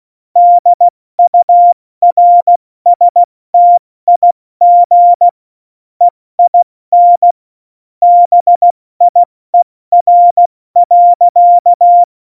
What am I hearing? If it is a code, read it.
Morse code, heard as a keyed tone, about 16 wpm, 700 Hz. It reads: DURSTIG EIN BIER.
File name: Morsekurs / Witze / Witz 22. Dezember 2020, 70 F80